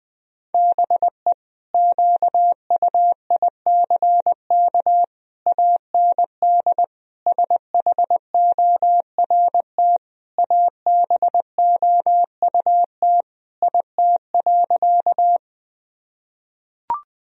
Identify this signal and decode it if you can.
Morse code, heard as a keyed tone, about 20 wpm, 700 Hz. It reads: BE QUICK AND SHORT ABOUT IT.
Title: Sentences from Top 1000 Words - Review Part 2 - 20wpm